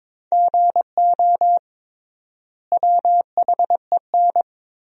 Morse code, heard as a keyed tone, about 22 wpm, 700 Hz. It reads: GO WHEN
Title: 3x Word-Spacing - Sentences from Top 500 Words - 22wpm